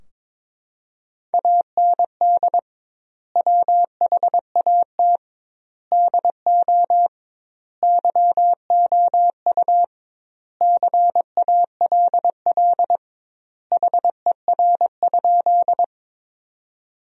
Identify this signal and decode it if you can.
Morse code, heard as a keyed tone, about 22 wpm, 700 Hz. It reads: AND WHAT DO YOU CALL HER?
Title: 2x Word-Spacing - Sentences from Top 100 Words - 22wpm